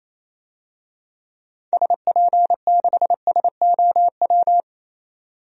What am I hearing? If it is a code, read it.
Morse code, heard as a keyed tone, about 28 wpm, 700 Hz. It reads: SP6SOW
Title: DXCC-100 Call Signs - 6 Characters - 28wpm